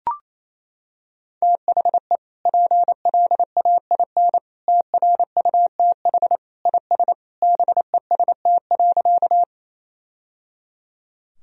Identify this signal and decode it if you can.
Morse code, heard as a keyed tone, about 28 wpm, 700 Hz. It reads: THE PLAIN TRUTH IS BEST.
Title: Sentences from Top 1000 Words - Review Part 2 - 28wpm